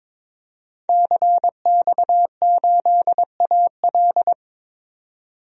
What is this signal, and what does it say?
Morse code, heard as a keyed tone, about 22 wpm, 700 Hz. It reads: CX8AL
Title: DXCC-100 Call Signs - 5 Characters - 22wpm